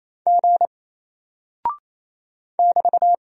Morse code, heard as a keyed tone, about 28 wpm, 700 Hz. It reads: G <BT>
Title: Single Character (Letter:Number:Punctuation:Prosign) Alt 28wpm